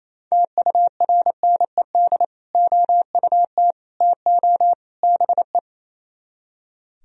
Morse code, heard as a keyed tone, about 34 wpm, 700 Hz. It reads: TURNED OUT TO BE
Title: Speed-Racing - Sets of 4 Words 28wpm